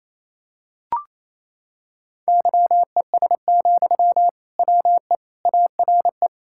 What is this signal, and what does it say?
Morse code, heard as a keyed tone, about 27 wpm, 700 Hz. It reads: YES, WE ARE
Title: Sentences from Top 1000 Words - Review Part 5 - 28wpm